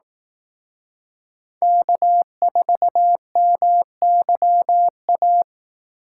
Morse code, heard as a keyed tone, about 18 wpm, 700 Hz. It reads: K4MYA
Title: Speed Racing - USA 1x3 Call Signs 15wpm